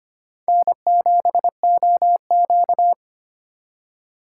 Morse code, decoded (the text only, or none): N7OQ